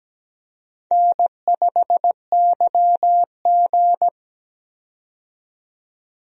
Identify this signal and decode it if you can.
Morse code, heard as a keyed tone, about 17 wpm, 700 Hz. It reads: N5YG
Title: USA Call Signs - 4 Characters - 17wpm